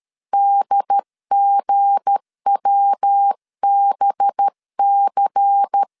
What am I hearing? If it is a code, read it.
Morse code, heard as a keyed tone, about 13 wpm, 800 Hz. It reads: DGWBC
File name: LEZIONI / 7 GF9Z / 7 G 12wpm 2055